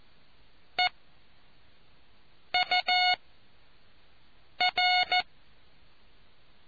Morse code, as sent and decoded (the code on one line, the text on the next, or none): . / ..- / .-.
E U R